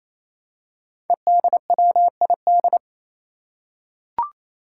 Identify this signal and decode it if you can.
Morse code, heard as a keyed tone, about 28 wpm, 700 Hz. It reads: EDWID
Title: Five Challenging Letter-Number Combinations 28wpm